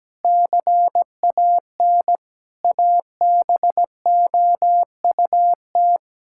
Morse code, heard as a keyed tone, about 17 wpm, 700 Hz. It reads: CAN ABOUT
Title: Sentences from Top 1000 Words - Review Part 5 - 17wpm